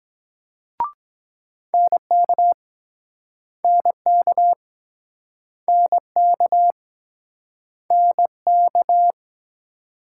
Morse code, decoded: NK NK NK NK

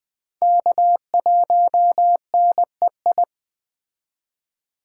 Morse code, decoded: K1NEI

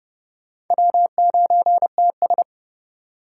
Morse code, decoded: W9TS

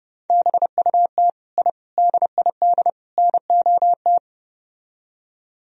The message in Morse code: BUT I DID NOT

